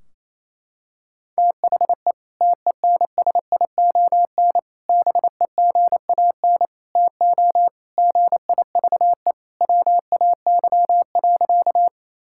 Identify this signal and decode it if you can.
Morse code, heard as a keyed tone, about 28 wpm, 700 Hz. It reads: THE TENSION BEGAN TO GIVE WAY.
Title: Sentences from Top 1600 Words 28wpm